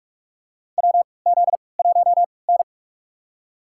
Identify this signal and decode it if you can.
Morse code, heard as a keyed tone, about 43 wpm, spaced as for 25 wpm, 700 Hz. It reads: WZ1N
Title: USA Call Signs - 3x Character Spacing - 45wpm